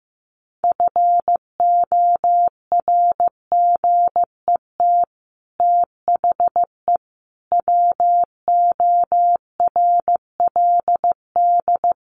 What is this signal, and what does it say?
Morse code, heard as a keyed tone, about 15 wpm, 700 Hz. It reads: FORGET THE WORLD
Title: Sentences from Top 1200 Words 15wpm